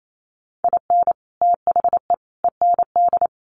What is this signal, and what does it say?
Morse code, heard as a keyed tone, about 28 wpm, 700 Hz. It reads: IN THE END
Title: Sets of 3 Words 28wpm